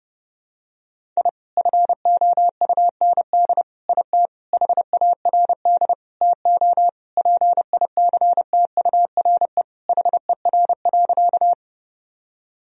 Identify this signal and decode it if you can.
Morse code, heard as a keyed tone, about 32 wpm, 700 Hz. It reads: I FOUND IT HARD TO PICTURE HER.
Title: Sentences from Top 1000 Words - Review Part 1 - 30wpm